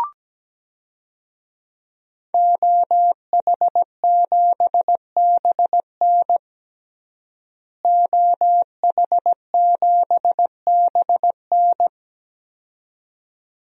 Morse code, heard as a keyed tone, about 17 wpm, 700 Hz. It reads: OH7BN OH7BN
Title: DXCC-100 Call Signs - 5 Characters - Repeated 2x - 17wpm